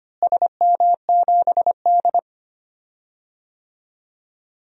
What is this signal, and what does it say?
Morse code, heard as a keyed tone, about 25 wpm, 700 Hz. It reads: SM7D
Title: DXCC-100 Call Signs - 4 Characters - Repeated 2x - 25wpm